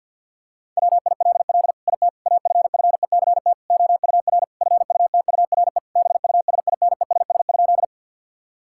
Morse code, heard as a keyed tone, about 50 wpm, 700 Hz. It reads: WILL IT AFFECT OUR FUTURE BUSINESS?
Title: Sentences from Top 1000 Words - Review Part 6 - 50wpm